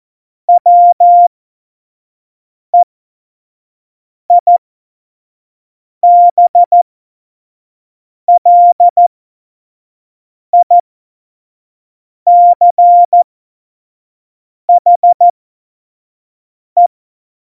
Morse code, heard as a keyed tone, about 14 wpm, spaced as for 5 wpm, 700 Hz. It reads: WEIBLICHE
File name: Morsekurs / Witze / Witz 19 Mai 2020, 25 F70